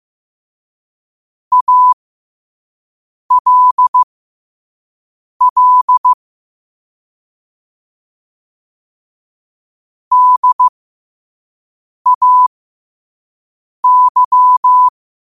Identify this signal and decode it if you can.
Morse code, heard as a keyed tone, about 15 wpm, spaced as for 5 wpm, 1000 Hz. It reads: ALL DAY
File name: qotdfiles / 5wpm